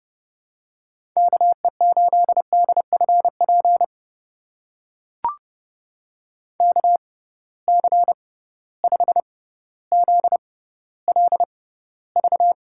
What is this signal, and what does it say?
Morse code, heard as a keyed tone, about 30 wpm, spaced as for 15 wpm, 700 Hz. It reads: KE8DFP KC5ZLV
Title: USA Call Signs - 6 Characters - 6x Character Spacing - 30wpm